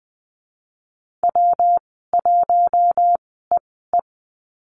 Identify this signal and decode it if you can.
Morse code, heard as a keyed tone, about 20 wpm, spaced as for 14 wpm, 700 Hz. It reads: W1EE
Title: 140 Lesson 23g - Letter f - Call Signs 20wpm